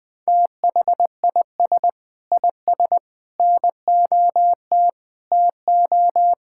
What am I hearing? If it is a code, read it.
Morse code, heard as a keyed tone, about 20 wpm, 700 Hz. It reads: THIS IS NOT TO
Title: Sentences with 6-Word Phrases - 20wpm